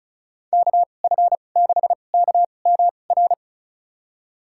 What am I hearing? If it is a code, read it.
Morse code, heard as a keyed tone, about 34 wpm, spaced as for 25 wpm, 700 Hz. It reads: KF6KMR
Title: USA Call Signs - 6 Characters - 2x Character Spacing - 35wpm